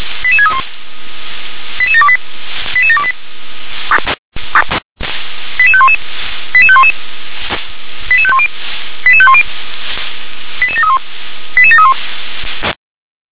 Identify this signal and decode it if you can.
Five-tone selective call (ZVEI), heard as quick runs of five tones, tones of about 70 ms. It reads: 88631 88728 88728 88511 88511 88510 88510 88731 88731